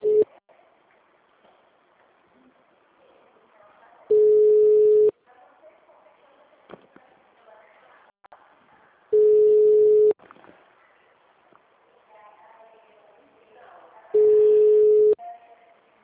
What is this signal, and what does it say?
Ringback tone (Europe, 425 Hz), with speech.